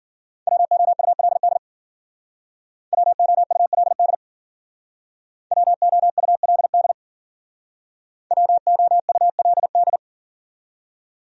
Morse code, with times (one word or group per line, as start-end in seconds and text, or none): 0.5-1.6 WOULD
2.9-4.1 WOULD
5.5-6.9 WOULD
8.3-10.0 WOULD